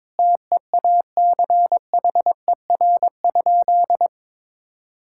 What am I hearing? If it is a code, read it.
Morse code, heard as a keyed tone, about 22 wpm, 700 Hz. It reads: TEACHER?
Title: Sentences from Top 1000 Words - Review Part 6 - 22wpm